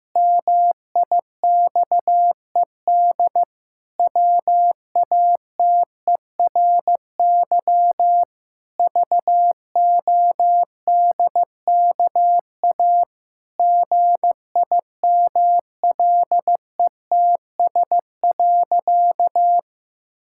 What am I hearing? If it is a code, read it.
Morse code, heard as a keyed tone, about 15 wpm, 700 Hz. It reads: MIXED WATERY VODKA GIMLETS.
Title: Pangram Challenge v2 - 15wpm